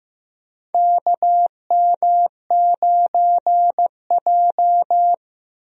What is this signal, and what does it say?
Morse code, heard as a keyed tone, about 15 wpm, 700 Hz. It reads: KM9J
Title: USA Call Signs - 4 Characters - Repeated 3x - 15wpm